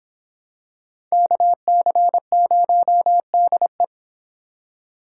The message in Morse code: KC0DE

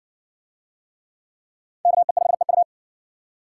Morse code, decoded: KE5EU

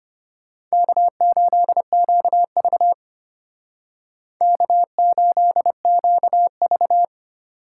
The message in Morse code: K8QV K8QV

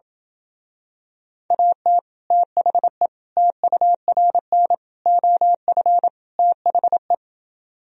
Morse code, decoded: AT THE TURN OF THE

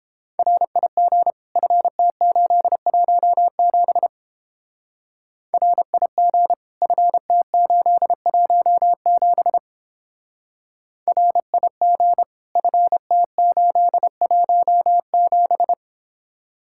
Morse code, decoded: RIG FT817 RIG FT817 RIG FT817

